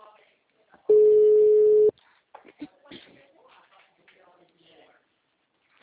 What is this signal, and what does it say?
Ringback tone (Europe, 425 Hz), with speech.